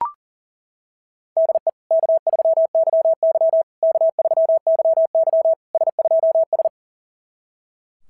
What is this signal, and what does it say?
Morse code, heard as a keyed tone, about 40 wpm, 660 Hz. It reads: DE K3YY K3YY S2S
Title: SOTA QSO Practice 40wpm